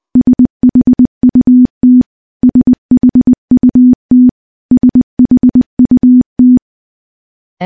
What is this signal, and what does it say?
Morse code, heard as a keyed tone, about 20 wpm, 250 Hz. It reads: SHUT SHUT SHUT
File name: morse / morse-training-20WPM